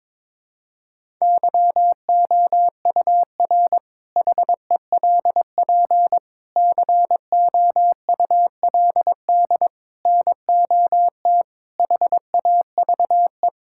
Morse code, heard as a keyed tone, about 22 wpm, 700 Hz. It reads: YOUR HELP COULD NOT HAVE
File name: practice / 22wpm / Sentences with 8-Word Phrases - 22wpm